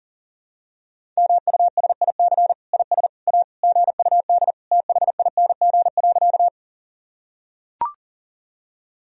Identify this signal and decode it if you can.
Morse code, heard as a keyed tone, about 40 wpm, 700 Hz. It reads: MUSIC IS A GUD THING.
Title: Sentences from Top 300 Words and Abbreviations 40wpm